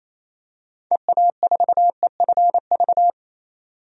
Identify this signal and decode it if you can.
Morse code, heard as a keyed tone, about 28 wpm, 700 Hz. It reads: EA4EFV
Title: DXCC-100 Call Signs - 6 Characters - 28wpm